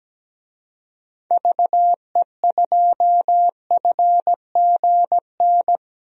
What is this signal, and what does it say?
Morse code, heard as a keyed tone, about 17 wpm, 700 Hz. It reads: VE2FGN
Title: DXCC-100 Call Signs - 6 Characters - 17wpm